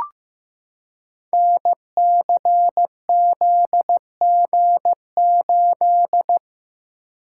Morse code, heard as a keyed tone, about 15 wpm, 700 Hz. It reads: NCZG8
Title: Code Groups - 5 Letter:Numbers - 15wpm